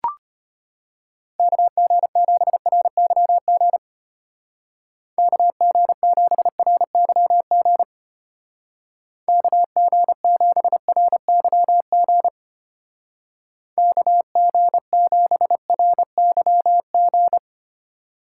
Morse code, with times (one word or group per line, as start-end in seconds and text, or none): 1.4-3.8 KG7RYG
5.2-7.8 KG7RYG
9.3-12.3 KG7RYG
13.8-17.4 KG7RYG